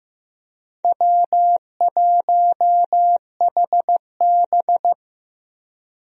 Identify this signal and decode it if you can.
Morse code, heard as a keyed tone, about 15 wpm, 700 Hz. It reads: W1HB
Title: USA Call Signs - 4 Characters - Repeated 3x - 15wpm